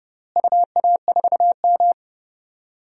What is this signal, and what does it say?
Morse code, heard as a keyed tone, about 30 wpm, 700 Hz. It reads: UA4M
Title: DXCC-100 Call Signs - 4 Characters - Repeated 2x - 30wpm